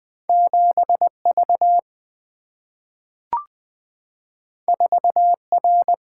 Morse code, heard as a keyed tone, about 20 wpm, 690 Hz. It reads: 7V 4R
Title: Two Letter-Number Combinations - 20wpm